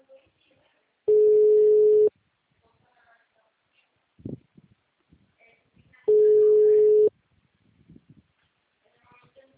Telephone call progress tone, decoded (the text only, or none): ringback tone